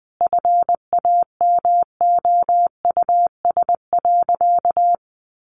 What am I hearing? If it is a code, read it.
Morse code, heard as a keyed tone, about 20 wpm, 700 Hz. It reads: FAMOUS.
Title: Sentences from Top 1500 Words 20wpm